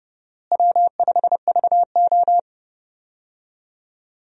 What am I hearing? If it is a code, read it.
Morse code, heard as a keyed tone, about 30 wpm, 700 Hz. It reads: W5VO
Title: USA Call Signs - 4 Characters - Repeated 2x - 30wpm